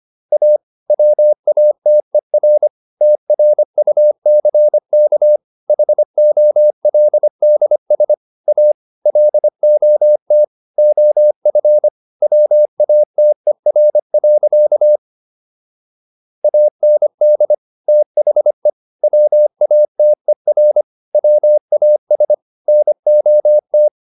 Morse code, heard as a keyed tone, about 25 wpm, 600 Hz. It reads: A WATER TRUCK HOLDS A LOT OF WATER. AND THE WATER WAS NOT